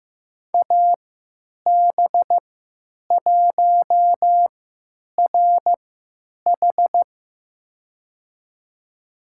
Morse code, heard as a keyed tone, about 15 wpm, spaced as for 9 wpm, 700 Hz. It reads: AB1RH
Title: USA Call Signs - 5 Characters - 3x Character Spacing - 15wpm